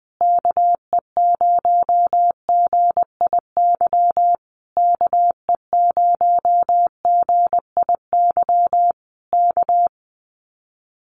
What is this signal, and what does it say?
Morse code, heard as a keyed tone, about 20 wpm, 700 Hz. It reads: KE0GIY KE0GIY K
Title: Common QSO Phrases Encore 20wpm